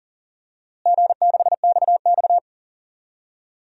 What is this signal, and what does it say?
Morse code, heard as a keyed tone, about 37 wpm, 700 Hz. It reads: G6XX
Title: DXCC-100 Call Signs - 4 Characters - Repeated 2x - 40wpm